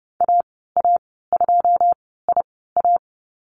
Morse code, heard as a keyed tone, about 31 wpm, spaced as for 17 wpm, 690 Hz. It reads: AA2IA
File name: Lesson 15g - Number 5 - Call Signs 30wpm